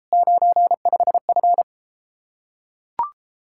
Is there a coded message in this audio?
Yes: Morse code from 0.1 to 1.6 s, heard as a keyed tone, about 33 wpm, 700 Hz. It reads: 95F